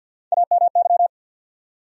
Morse code, heard as a keyed tone, about 48 wpm, 700 Hz. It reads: AMY
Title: USA Common Names - 50wpm